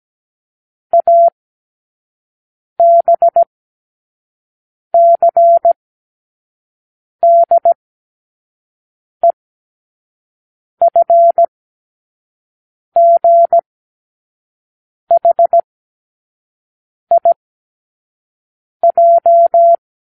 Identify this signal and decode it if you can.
Morse code, heard as a keyed tone, about 17 wpm, spaced as for 5 wpm, 680 Hz. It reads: ABCDEFGHIJ
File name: k7qo / Lesson15(A-J)